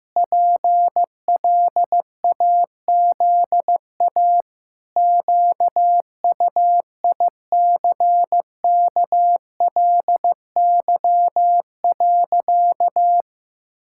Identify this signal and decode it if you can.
Morse code, heard as a keyed tone, about 15 wpm, 700 Hz. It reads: PLAZA QUICKLY.
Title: Pangram Challenge v2 - 15wpm